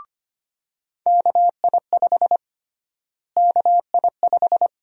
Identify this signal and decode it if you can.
Morse code, heard as a keyed tone, about 25 wpm, 700 Hz. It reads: KI5 KI5
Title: USA Call Sign Prefixes - Repeated 2x - 25wpm